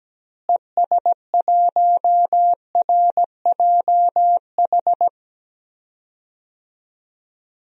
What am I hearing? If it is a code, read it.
Morse code, heard as a keyed tone, about 17 wpm, 700 Hz. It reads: ES1RJH
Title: DXCC-100 Call Signs - 6 Characters - Repeated 2x - 17wpm